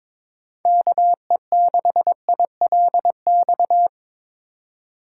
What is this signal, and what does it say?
Morse code, heard as a keyed tone, about 22 wpm, 700 Hz. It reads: KE6ILX
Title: USA Call Signs - 6 Characters - 4x Character Spacing - 22wpm